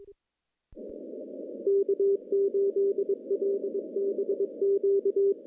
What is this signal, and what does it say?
Morse code, heard as a keyed tone, about 23 wpm, 400 Hz. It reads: K8LBQ
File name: contests / CW-SS-k5tr- -0200